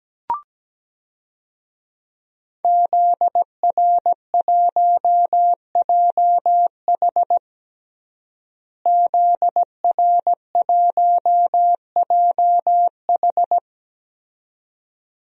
Morse code, heard as a keyed tone, about 17 wpm, 700 Hz. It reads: ZR1JH ZR1JH
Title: DXCC-100 Call Signs - 5 Characters - Repeated 2x - 17wpm